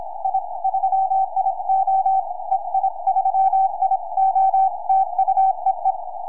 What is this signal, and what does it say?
Morse code, heard as a keyed tone, about 25 wpm, 770 Hz. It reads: I 3IO EI3IO TU EE
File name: ei3io 4m